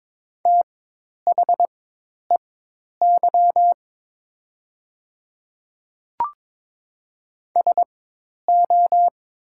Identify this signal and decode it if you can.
Morse code, heard as a keyed tone, about 22 wpm, spaced as for 10 wpm, 700 Hz. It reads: THEY SO
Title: 4x Character Spacing - Top 200 Words - 22wpm